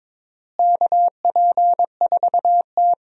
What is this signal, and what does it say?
Morse code, heard as a keyed tone, about 22 wpm, 700 Hz. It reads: KP4T
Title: DXCC-100 Call Signs - 4 Characters - Repeated 3x - 22wpm